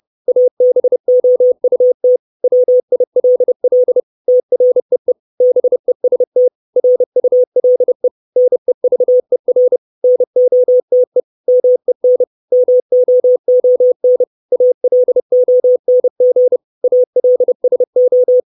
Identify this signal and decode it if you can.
Morse code, heard as a keyed tone, about 30 wpm, 500 Hz. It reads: ABOUT WILL TREE BEST RULE NEVER NOTE MEN MOON ALONG ALSO
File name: 632 Words 30WPM